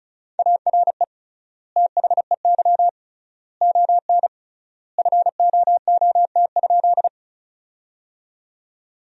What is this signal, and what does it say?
Morse code, heard as a keyed tone, about 36 wpm, 700 Hz. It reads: ARE THEY ON FOOT?
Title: 3x Word-Spacing - Sentences from Top 500 Words - 35wpm